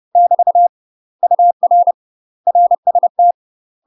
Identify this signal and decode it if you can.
Morse code, heard as a keyed tone, about 30 wpm, 700 Hz. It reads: <BT> UR RST
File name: UK MASTER QSOs TRAINING / 30WPM / UK MASTER QSOs 0021